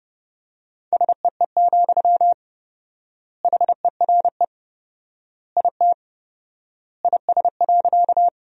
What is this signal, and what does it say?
Morse code, heard as a keyed tone, about 30 wpm, 700 Hz. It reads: SEE, HERE IT IS.
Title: 4x Word-Spacing - Sentences from Top 200 Words - 30wpm